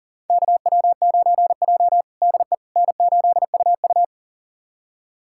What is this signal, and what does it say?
Morse code, heard as a keyed tone, about 37 wpm, 700 Hz. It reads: KW9J DE N8UU
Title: USA Callsign DE Callsign - 4 Characters - 40wpm